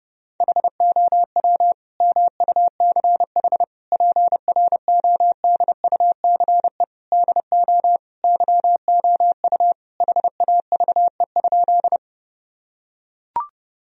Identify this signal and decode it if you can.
Morse code, heard as a keyed tone, about 32 wpm, 700 Hz. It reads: HOW MUCH PRODUCE DO YOU HAVE?
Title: Sentences from Top 1000 Words - Review Part 3 - 30wpm